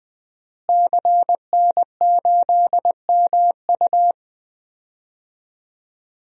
Morse code, heard as a keyed tone, about 21 wpm, 700 Hz. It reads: CN8MU
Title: DXCC-100 Call Signs - 5 Characters - Repeated 3x - 20wpm